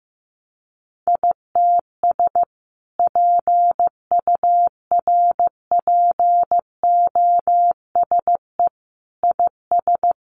Morse code, heard as a keyed tone, about 15 wpm, 700 Hz. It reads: ITS PURPOSE IS
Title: A Sound Of Thunder 15wpm